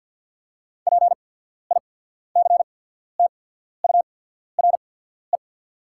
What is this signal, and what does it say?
Morse code, heard as a keyed tone, about 48 wpm, spaced as for 14 wpm, 700 Hz. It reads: PICTURE
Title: 8x Character Spacing - Top 300 Words - 50wpm